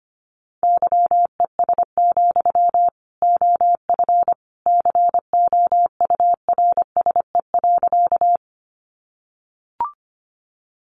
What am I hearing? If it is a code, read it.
Morse code, heard as a keyed tone, about 25 wpm, 700 Hz. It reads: YES, OF COURSE.
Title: Sentences from Top 500 Words 25wpm